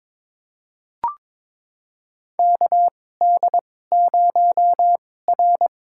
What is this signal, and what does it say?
Morse code, heard as a keyed tone, about 22 wpm, spaced as for 16 wpm, 700 Hz. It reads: KD0R
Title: USA Call Signs - 4 Characters - 2x Character Spacing - 22wpm